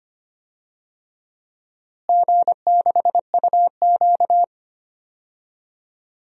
Morse code, heard as a keyed tone, about 25 wpm, 700 Hz. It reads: G6UQ
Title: DXCC-100 Call Signs - 4 Characters - Repeated 2x - 25wpm